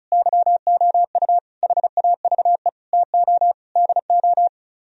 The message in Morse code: YOU HAVE TO DO

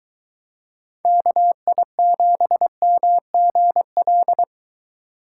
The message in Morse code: KI7MGL

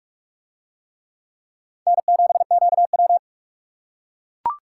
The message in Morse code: N7QW

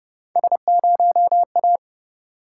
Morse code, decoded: S0A

